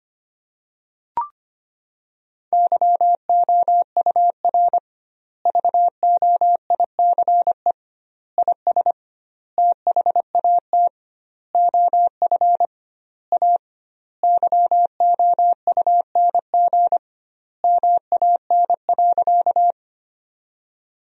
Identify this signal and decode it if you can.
Morse code, heard as a keyed tone, about 25 wpm, 700 Hz. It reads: YOUR VOICE IS THAT OF A YOUNG MAN.
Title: 2x Word-Spacing - Sentences from Top 500 Words - 25wpm